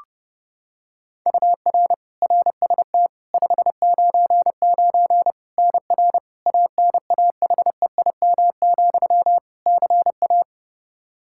Morse code, heard as a keyed tone, about 30 wpm, 700 Hz. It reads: UR RST 599 NR ANAHEIM, CA
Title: Practice QSO - Exchange 1 - Beginning - 30wpm